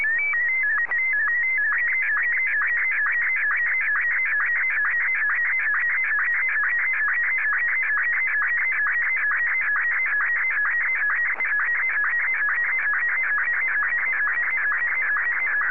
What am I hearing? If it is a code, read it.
SSTV (slow-scan television), Martin 1.